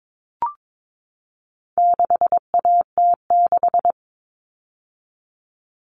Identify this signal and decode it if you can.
Morse code, heard as a keyed tone, about 22 wpm, 700 Hz. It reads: 6AT6
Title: All Cut Numbers - 4-Digit 22wpm